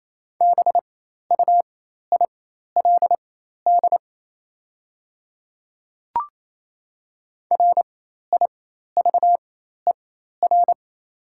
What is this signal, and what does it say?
Morse code, heard as a keyed tone, about 28 wpm, spaced as for 13 wpm, 690 Hz. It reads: BUILD RIVER
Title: 4x Character Spacing - Top 300 Words - 28wpm